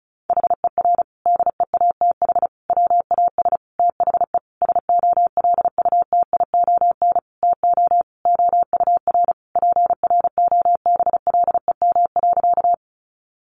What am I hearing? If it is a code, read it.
Morse code, heard as a keyed tone, about 34 wpm, 700 Hz. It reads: HER DEATH WAS THE SOLUTION TO OUR PROBLEM.
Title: Sentences from Top 1300 Words 35wpm